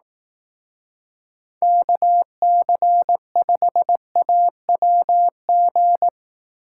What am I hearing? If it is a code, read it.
Morse code, heard as a keyed tone, about 18 wpm, 700 Hz. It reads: KC5AWG
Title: Speed Racing - USA Call Signs 15wpm